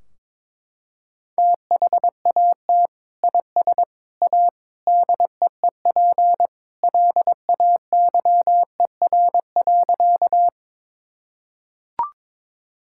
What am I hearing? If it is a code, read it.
Morse code, heard as a keyed tone, about 21 wpm, 700 Hz. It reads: THAT IS A DEEP LAYER.
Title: Sentences from Top 1600 Words 22wpm